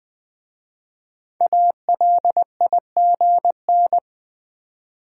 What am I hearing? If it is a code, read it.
Morse code, heard as a keyed tone, about 20 wpm, 700 Hz. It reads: ALIGN
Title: 5 Letter Words - First 3 Letters Given - 20wpm